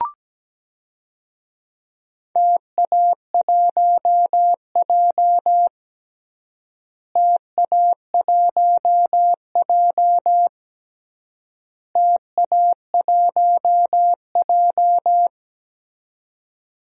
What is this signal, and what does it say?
Morse code, heard as a keyed tone, about 17 wpm, 700 Hz. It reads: TA1J TA1J TA1J
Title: DXCC-100 Call Signs - 4 Characters - Repeated 3x - 17wpm